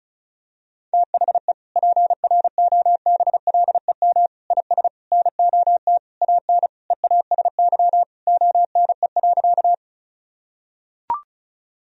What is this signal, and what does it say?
Morse code, heard as a keyed tone, about 34 wpm, 700 Hz. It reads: THE PROBLEM IS NOT AN EASY ONE.
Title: Sentences from Top 1000 Words - Review Part 2 - 35wpm